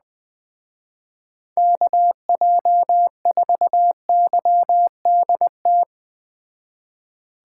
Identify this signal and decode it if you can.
Morse code, heard as a keyed tone, about 20 wpm, 700 Hz. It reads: KJ4YDT
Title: Speed Racing - USA 2x3 Call Signs 17wpm